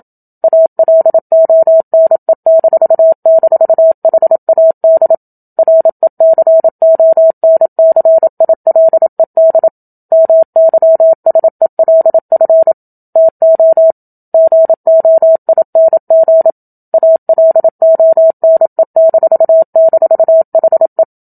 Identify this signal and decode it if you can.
Morse code, heard as a keyed tone, about 27 wpm, 650 Hz. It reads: ALONE--HAD RECONCILED MYSELF TO GOING ALONE--HE
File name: Book Two / 01-Chapter 1 - 27 WPM